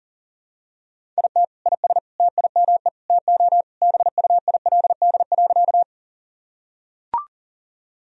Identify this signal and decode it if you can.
Morse code, heard as a keyed tone, about 40 wpm, 700 Hz. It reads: IT IS TIME TO BUILD.
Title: Sentences from Top 1000 Words - Review Part 1 - 40wpm